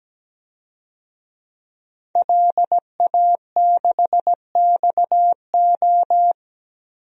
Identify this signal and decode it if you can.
Morse code, heard as a keyed tone, about 17 wpm, 700 Hz. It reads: LA6XO
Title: DXCC-100 Call Signs - 5 Characters - Repeated 3x - 17wpm